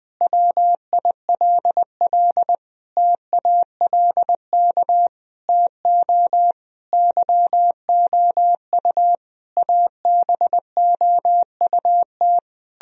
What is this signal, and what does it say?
Morse code, heard as a keyed tone, about 20 wpm, 700 Hz. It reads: WILL TALK TO YOU ABOUT
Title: Sentences from Top 1000 Words - Review Part 1 - 20wpm